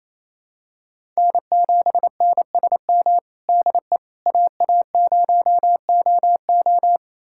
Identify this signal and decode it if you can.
Morse code, heard as a keyed tone, about 28 wpm, 700 Hz. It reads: N7NSM DE AA0OO
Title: USA Callsign DE Callsign - 5 Characters - 28wpm